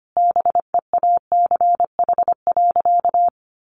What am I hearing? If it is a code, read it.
Morse code, heard as a keyed tone, about 25 wpm, 700 Hz. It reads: BEACH.
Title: Sentences from Top 1500 Words 25wpm